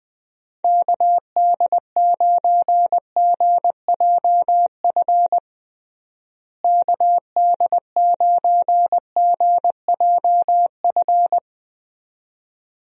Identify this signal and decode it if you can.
Morse code, heard as a keyed tone, about 20 wpm, 700 Hz. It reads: KD9GJF KD9GJF
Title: USA Call Signs - 6 Characters - Repeated 2x - 20wpm